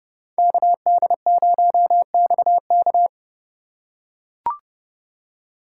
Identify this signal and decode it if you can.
Morse code, heard as a keyed tone, about 30 wpm, 700 Hz. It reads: KD0XK